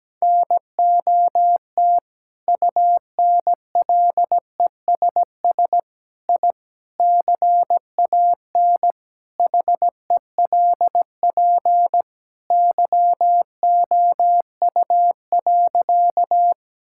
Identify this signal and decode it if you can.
Morse code, heard as a keyed tone, about 17 wpm, 700 Hz. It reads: NOT UNLESS I CAN HELP YOU.
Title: Sentences from Top 1000 Words - Review Part 6 - 17wpm